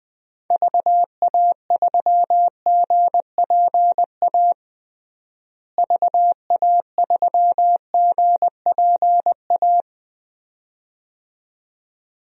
Morse code, heard as a keyed tone, about 20 wpm, 700 Hz. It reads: VA3GPA VA3GPA
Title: DXCC-100 Call Signs - 6 Characters - Repeated 2x - 20wpm